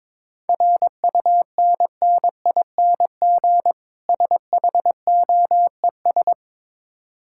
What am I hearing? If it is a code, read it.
Morse code, heard as a keyed tone, about 22 wpm, 690 Hz. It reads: RUNNING SHOES
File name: Difficult Words with Hints - 22wpm